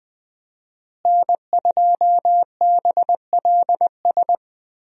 Morse code, heard as a keyed tone, about 21 wpm, 700 Hz. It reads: N2BLS
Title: USA Call Signs - 5 Characters - Repeated 3x - 20wpm